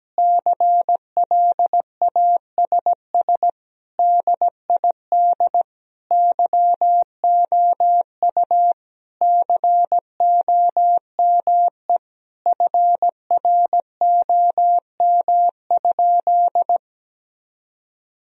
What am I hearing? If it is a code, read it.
Morse code, heard as a keyed tone, about 17 wpm, 700 Hz. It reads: CLASS DID YOU COME FROM?
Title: Sentences from Top 1000 Words - Review Part 2 - 17wpm